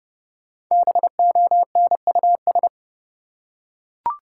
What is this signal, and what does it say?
Morse code, heard as a keyed tone, about 31 wpm, 700 Hz. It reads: BONUS
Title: Five Letter Words - 30wpm